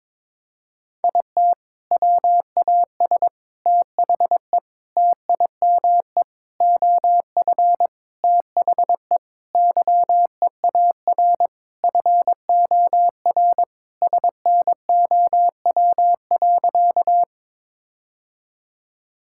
Morse code, heard as a keyed tone, about 22 wpm, 700 Hz. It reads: IT WAS THE TIME OF THE YEAR FOR SNOW.
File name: practice / 22wpm / Sentences with 4-Word Phrases - 22wpm